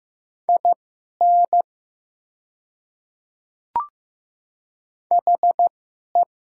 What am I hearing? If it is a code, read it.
Morse code, heard as a keyed tone, about 15 wpm, spaced as for 11 wpm, 690 Hz. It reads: IN HE
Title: 2x Character Spacing - Top 100 Words - 15wpm